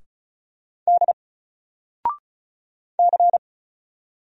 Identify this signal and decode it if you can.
Morse code, heard as a keyed tone, about 35 wpm, 700 Hz. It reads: D C
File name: Single-Letter-Number-35wpm